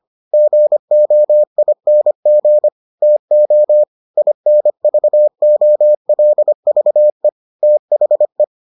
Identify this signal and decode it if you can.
Morse code, heard as a keyed tone, about 25 wpm, 600 Hz. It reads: GOING TO INVOLVE THE